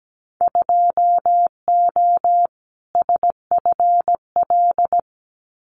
Morse code, heard as a keyed tone, about 17 wpm, 700 Hz. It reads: 2O SFL
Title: Winter Field Day Exchanges 17wpm